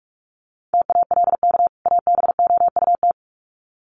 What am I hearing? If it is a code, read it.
Morse code, heard as a keyed tone, about 44 wpm, 700 Hz. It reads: TALK ABOUT
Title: Sets of 2 Words 45wpm